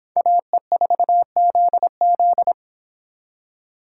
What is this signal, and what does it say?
Morse code, heard as a keyed tone, about 24 wpm, 700 Hz. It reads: AE4ZZ